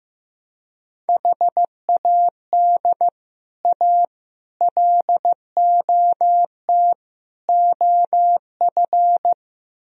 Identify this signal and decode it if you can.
Morse code, heard as a keyed tone, about 15 wpm, 700 Hz. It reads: HAD A LOT OF
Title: Speed-Racing - Sets of 4 Words 15wpm